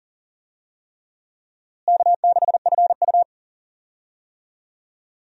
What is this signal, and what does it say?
Morse code, heard as a keyed tone, about 40 wpm, 700 Hz. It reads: K6FU
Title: USA Call Signs - 4 Characters - Repeated 3x - 40wpm